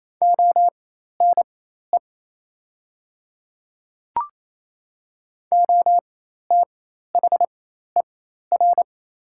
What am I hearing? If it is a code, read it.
Morse code, heard as a keyed tone, about 28 wpm, spaced as for 13 wpm, 700 Hz. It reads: ONE OTHER
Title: 4x Character Spacing - Top 100 Words - 28wpm